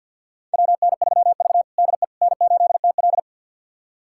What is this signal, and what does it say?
Morse code, heard as a keyed tone, about 51 wpm, 700 Hz. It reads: WN3V DE N8TL
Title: USA Callsign DE Callsign - 4 Characters - 50wpm